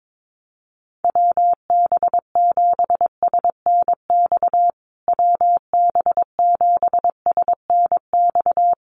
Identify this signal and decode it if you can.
Morse code, heard as a keyed tone, about 22 wpm, 700 Hz. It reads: WB7SNX WB7SNX
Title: USA Call Signs Repeated 3x 22wpm